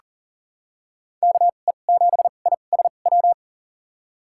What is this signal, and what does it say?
Morse code, heard as a keyed tone, about 39 wpm, spaced as for 29 wpm, 700 Hz. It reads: KE7ISW
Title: USA Call Signs - 6 Characters - 2x Character Spacing - 40wpm